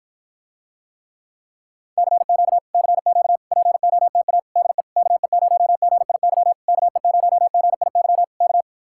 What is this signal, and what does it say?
Morse code, heard as a keyed tone, about 54 wpm, 700 Hz. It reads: CQ CQ POTA DE KE0GIY KE0GIY K